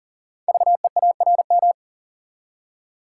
Morse code, heard as a keyed tone, about 42 wpm, 700 Hz. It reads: VEARM
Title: Code Groups - 5 Letters - 40wpm